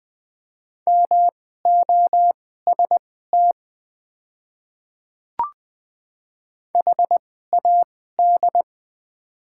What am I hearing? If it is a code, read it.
Morse code, heard as a keyed tone, about 21 wpm, spaced as for 15 wpm, 700 Hz. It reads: MOST HAD
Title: 2x Character Spacing - Top 100 Words - 20wpm